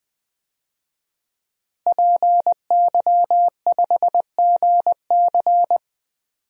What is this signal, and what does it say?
Morse code, heard as a keyed tone, about 20 wpm, 700 Hz. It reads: PY5GC
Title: DXCC-100 Call Signs - 5 Characters - Repeated 2x - 20wpm